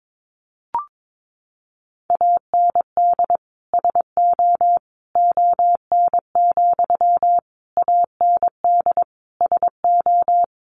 Morse code, heard as a keyed tone, about 22 wpm, 700 Hz. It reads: AND SO ON, AND SO
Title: Sentences from Top 100 Words 22wpm